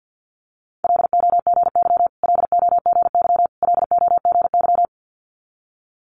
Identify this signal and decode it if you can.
Morse code, heard as a keyed tone, about 50 wpm, 700 Hz. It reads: LOGY LOGY LOGY
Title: Multi-letter Phonemes Repeated 3x 50wpm